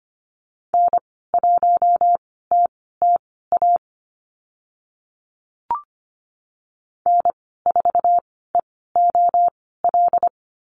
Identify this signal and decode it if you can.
Morse code, heard as a keyed tone, about 25 wpm, spaced as for 16 wpm, 700 Hz. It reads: N1TTA N4EOL
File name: Lesson 15g - Number 5 - Call Signs 25wpm